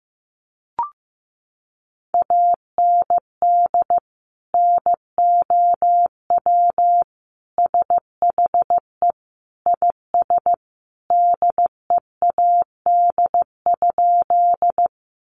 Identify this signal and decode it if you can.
Morse code, heard as a keyed tone, about 15 wpm, 700 Hz. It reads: AND NOW SHE IS DEAD?
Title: Sentences from Top 1500 Words 15wpm